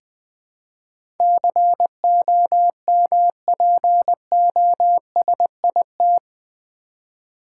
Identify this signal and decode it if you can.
Morse code, heard as a keyed tone, about 20 wpm, 700 Hz. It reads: COMPOSIT
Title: Build A Word - 4 Letters At A Time - 11-Letter Words - 20wpm